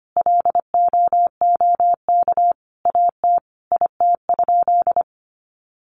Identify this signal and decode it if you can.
Morse code, heard as a keyed tone, about 25 wpm, 700 Hz. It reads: LOOK AT IT?
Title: Sentences from Top 100 Words 25wpm